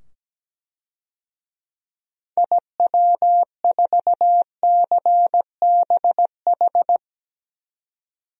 Morse code, heard as a keyed tone, about 17 wpm, 700 Hz. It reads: IW4CBH
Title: DXCC-100 Call Signs - 6 Characters - 17wpm